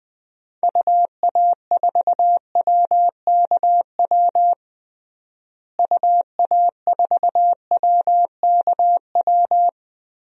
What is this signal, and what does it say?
Morse code, heard as a keyed tone, about 20 wpm, 700 Hz. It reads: UA4WKW UA4WKW
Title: DXCC-100 Call Signs - 6 Characters - Repeated 3x - 20wpm